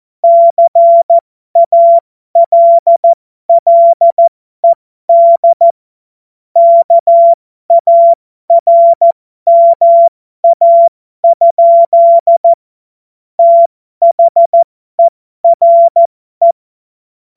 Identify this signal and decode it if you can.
Morse code, heard as a keyed tone, about 14 wpm, spaced as for 12 wpm, 680 Hz. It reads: CALLED KARMA? THERE
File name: Morsekurs / Witze